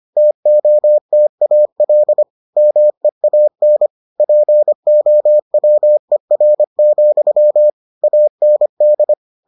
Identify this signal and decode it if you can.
Morse code, heard as a keyed tone, about 25 wpm, 600 Hz. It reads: TOTAL MEAN POWER, AND